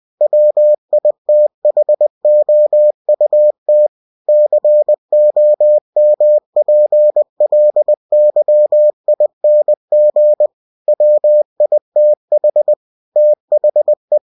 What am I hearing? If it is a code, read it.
Morse code, heard as a keyed tone, about 20 wpm, 600 Hz. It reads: WITHOUT COMPLYING WITH THE
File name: cw-books / winnie-the-pooh / winnie-20wpm-page-0113